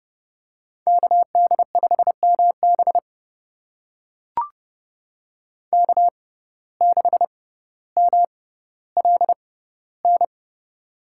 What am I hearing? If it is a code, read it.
Morse code, heard as a keyed tone, about 29 wpm, spaced as for 15 wpm, 700 Hz. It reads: KD5MB K6MLN
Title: USA Call Signs - 5 Characters - 6x Character Spacing - 30wpm